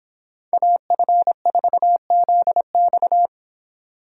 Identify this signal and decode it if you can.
Morse code, heard as a keyed tone, about 25 wpm, 700 Hz. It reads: AF4ZX